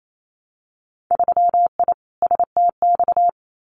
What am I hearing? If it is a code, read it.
Morse code, heard as a keyed tone, about 28 wpm, 700 Hz. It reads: 3I STX
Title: Winter Field Day Exchanges 28wpm